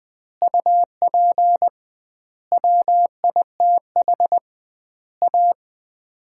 Morse code, heard as a keyed tone, about 20 wpm, 700 Hz. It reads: UP WITH A
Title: 2x Word-Spacing - Sets of 5 Words - 20wpm